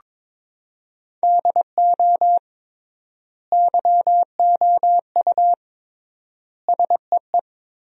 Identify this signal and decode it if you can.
Morse code, heard as a keyed tone, about 21 wpm, 700 Hz. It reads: DO YOU SEE
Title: 3x Word-Spacing - Sentences from Top 500 Words - 22wpm